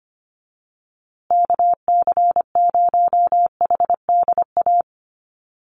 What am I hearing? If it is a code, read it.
Morse code, heard as a keyed tone, about 25 wpm, 700 Hz. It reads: KC0HDA
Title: USA Call Signs Repeated 2x 25wpm